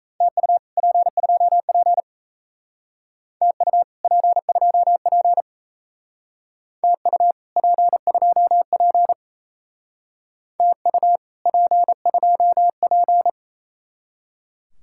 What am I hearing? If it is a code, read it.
Morse code, heard as a keyed tone, about 36 wpm, 700 Hz. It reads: TU P2P TU P2P TU P2P TU P2P